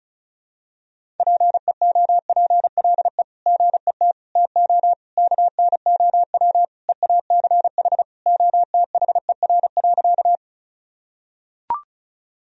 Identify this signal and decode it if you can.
Morse code, heard as a keyed tone, about 36 wpm, 700 Hz. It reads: PEOPLE GET TO KNOW EACH OTHER.
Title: Sentences from Top 1000 Words - Review Part 1 - 35wpm